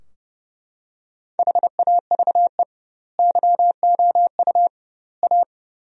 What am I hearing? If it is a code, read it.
Morse code, heard as a keyed tone, about 31 wpm, 700 Hz. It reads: HAVE YOU A
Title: 2x Word-Spacing - Sentences from Top 400 Words - 30wpm